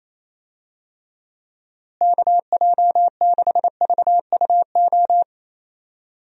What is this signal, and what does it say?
Morse code, heard as a keyed tone, about 28 wpm, 700 Hz. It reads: KJ6VUO
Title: USA Call Signs - 6 Characters - Repeated 3x - 28wpm